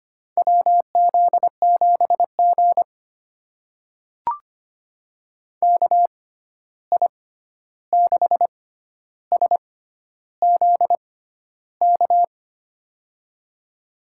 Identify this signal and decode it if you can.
Morse code, heard as a keyed tone, about 25 wpm, spaced as for 11 wpm, 700 Hz. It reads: WZ7G KI6SZK
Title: USA Call Signs - 6x Character Spacing - 25wpm